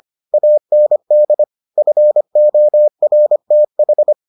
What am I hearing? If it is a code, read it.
Morse code, heard as a keyed tone, about 25 wpm, 600 Hz. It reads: AND FORTH